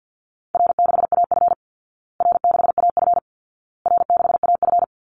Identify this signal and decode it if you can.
Morse code, heard as a keyed tone, about 48 wpm, 700 Hz. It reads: R6AF R6AF R6AF
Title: Call Signs Repeated 2x - 50wpm